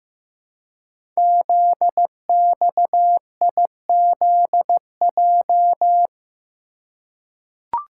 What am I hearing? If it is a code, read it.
Morse code, heard as a keyed tone, about 15 wpm, 700 Hz. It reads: ZXIZJ
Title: Code Groups - 5 Letters - 15wpm